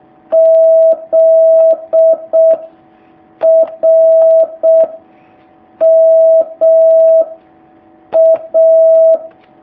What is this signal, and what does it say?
Morse code, heard as a keyed tone, about 6 wpm, 650 Hz. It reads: Z R M A
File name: jh8cbh / 2019cw